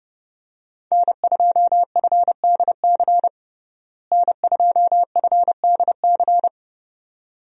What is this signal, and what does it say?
Morse code, heard as a keyed tone, about 30 wpm, 700 Hz. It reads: N2FDC N2FDC
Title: USA Call Signs - 5 Characters - Repeated 2x - 30wpm